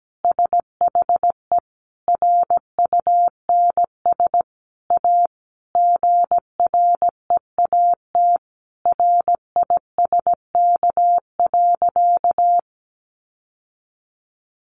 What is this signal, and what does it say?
Morse code, heard as a keyed tone, about 17 wpm, 700 Hz. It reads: SHE RUNS A GREAT RISK.
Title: Sentences from Top 600 Words 17wpm